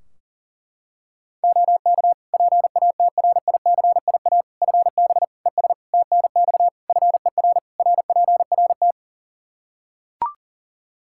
Morse code, heard as a keyed tone, about 37 wpm, 700 Hz. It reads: OK PATRICIA FB ES TNX FER RPRT
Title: Practice QSO - Exchange 2 - Info - 40wpm